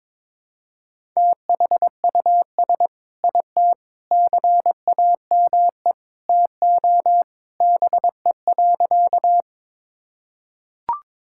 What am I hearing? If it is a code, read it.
Morse code, heard as a keyed tone, about 22 wpm, 700 Hz. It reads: THUS IT CAME TO BE.
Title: Sentences from Top 1000 Words - Review Part 5 - 22wpm